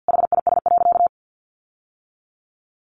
Morse code, heard as a keyed tone, about 49 wpm, 700 Hz. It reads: HIS.
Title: Sentences from Top 900 Words 50wpm